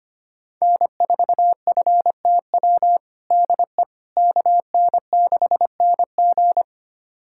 Morse code, heard as a keyed tone, about 25 wpm, 700 Hz. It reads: N4FTW DE KN6NG
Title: USA Callsign DE Callsign - 5 Characters - 25wpm